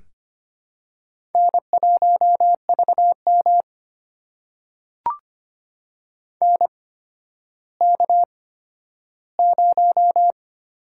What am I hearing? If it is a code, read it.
Morse code, heard as a keyed tone, about 25 wpm, spaced as for 12 wpm, 700 Hz. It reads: N1VM NK0
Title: USA Call Signs - 4 Characters - 8x Character Spacing - 25wpm